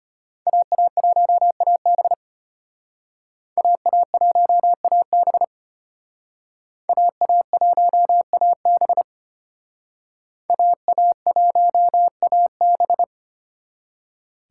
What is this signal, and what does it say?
Morse code, heard as a keyed tone, about 33 wpm, 700 Hz. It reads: AA1AB AA1AB AA1AB AA1AB